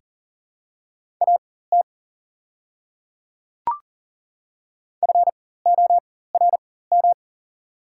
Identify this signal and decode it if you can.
Morse code, heard as a keyed tone, about 40 wpm, spaced as for 19 wpm, 700 Hz. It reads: AT FORM
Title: 4x Character Spacing - Top 200 Words - 40wpm